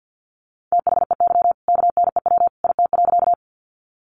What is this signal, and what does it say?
Morse code, heard as a keyed tone, about 48 wpm, 700 Hz. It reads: THEY KNEW IT.
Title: Sentences from Top 400 Words 50wpm